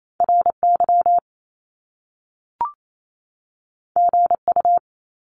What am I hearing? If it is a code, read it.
Morse code, heard as a keyed tone, about 28 wpm, 700 Hz. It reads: RY GU
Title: Top Two Letter Combinations in Words 28wpm